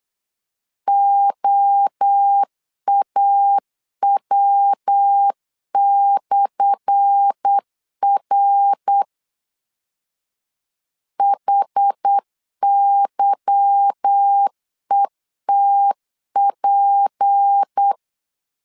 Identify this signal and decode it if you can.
Morse code, heard as a keyed tone, about 8 wpm, 800 Hz. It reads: OAW/R HYETP
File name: LEZIONI / 10 SIGNS / 10A SLASH 8wpm 3452